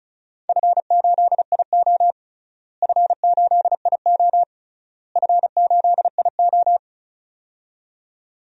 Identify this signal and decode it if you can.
Morse code, heard as a keyed tone, about 34 wpm, 700 Hz. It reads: F8IO F8IO F8IO
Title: DXCC-100 Call Signs - 4 Characters - Repeated 3x - 35wpm